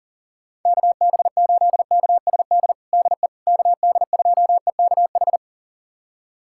Morse code, heard as a keyed tone, about 40 wpm, 700 Hz. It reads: KB8KSD DE KD2EKH
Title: USA Callsign DE Callsign 40wpm